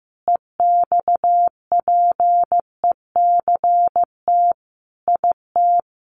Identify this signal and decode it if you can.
Morse code, heard as a keyed tone, about 15 wpm, 700 Hz. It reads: EXPECT IT
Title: Sentences from Top 1100 Words 15wpm